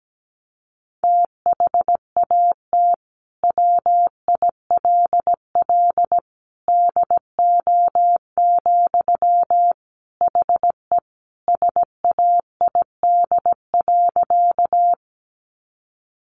Morse code, heard as a keyed tone, about 17 wpm, 700 Hz. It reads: THAT WILL DO, HE SAID.
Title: Sentences from Top 100 Words 17wpm